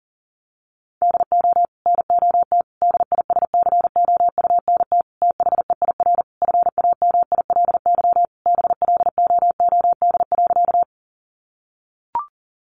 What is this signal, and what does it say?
Morse code, heard as a keyed tone, about 40 wpm, 700 Hz. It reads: DO NOT DISCOUNT THEIR FAMILY BLOOD.
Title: Sentences from Top 1200 Words 40wpm